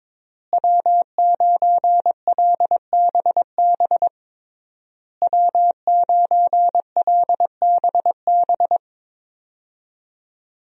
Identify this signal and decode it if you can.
Morse code, heard as a keyed tone, about 22 wpm, 700 Hz. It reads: W9LBB W9LBB
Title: USA Call Signs - 5 Characters - Repeated 3x - 22wpm